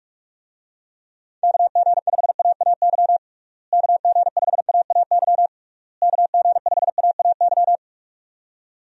Morse code, heard as a keyed tone, about 47 wpm, 700 Hz. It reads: KG5AAY KG5AAY KG5AAY
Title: USA Call Signs - 6 Characters - Repeated 3x - 45wpm